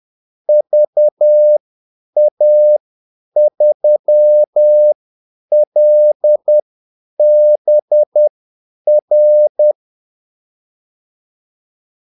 Morse code, heard as a keyed tone, about 10 wpm, spaced as for 8 wpm, 600 Hz. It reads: VA3LBR
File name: Callsigns-10-8wpm